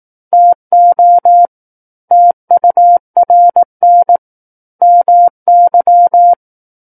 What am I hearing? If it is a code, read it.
Morse code, heard as a keyed tone, about 18 wpm, 700 Hz. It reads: TO TURN MY
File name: Book One / 09-Chapter 9 - 18 WPM